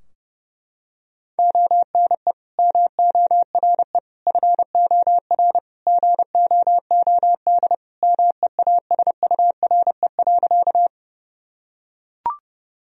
Morse code, heard as a keyed tone, about 32 wpm, 700 Hz. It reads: ONE MORE FOR GOOD MEASURE.
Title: Sentences from Top 1000 Words - Review Part 2 - 30wpm